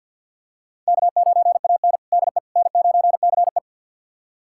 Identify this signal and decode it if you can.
Morse code, heard as a keyed tone, about 46 wpm, 700 Hz. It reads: K9AN DE N9CE
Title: USA Callsign DE Callsign - 4 Characters - 50wpm